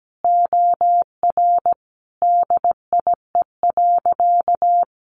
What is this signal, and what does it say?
Morse code, heard as a keyed tone, about 17 wpm, 700 Hz. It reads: OR DIE.
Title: Sentences from Top 1300 Words 17wpm